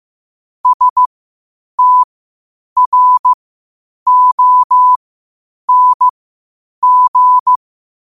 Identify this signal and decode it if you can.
Morse code, heard as a keyed tone, about 15 wpm, spaced as for 8 wpm, 1000 Hz. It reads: STRONG
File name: qotdfiles / 8wpm